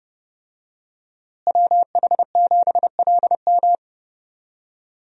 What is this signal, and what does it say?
Morse code, heard as a keyed tone, about 28 wpm, 700 Hz. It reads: WH7LM
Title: USA Call Signs - 5 Characters - Repeated 3x - 30wpm